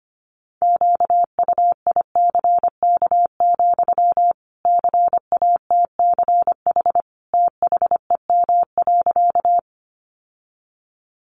Morse code, heard as a keyed tone, about 25 wpm, 700 Hz. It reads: QUICK, CATCH THEM.
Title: Sentences from Top 1500 Words 25wpm